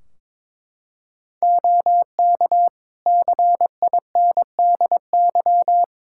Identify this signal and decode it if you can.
Morse code, heard as a keyed tone, about 22 wpm, 700 Hz. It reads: OK CINDY
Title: K1USN SST QSO Practice 22wpm